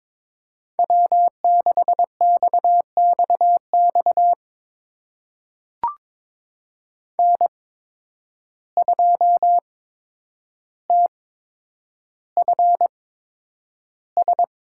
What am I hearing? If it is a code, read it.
Morse code, heard as a keyed tone, about 22 wpm, spaced as for 9 wpm, 700 Hz. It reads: W6XXX N2TFS
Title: USA Call Signs - 5 Characters - 8x Character Spacing - 22wpm